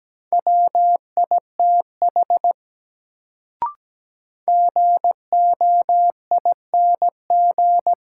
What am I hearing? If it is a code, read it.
Morse code, heard as a keyed tone, about 17 wpm, 700 Hz. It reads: WITH GOING
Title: Sets of 2 Words - New Word Drill - 17wpm